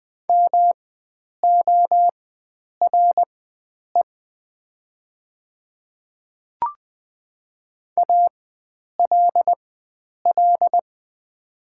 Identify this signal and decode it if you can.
Morse code, heard as a keyed tone, about 20 wpm, spaced as for 9 wpm, 700 Hz. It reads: MORE ALL
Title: 4x Character Spacing - Top 100 Words - 20wpm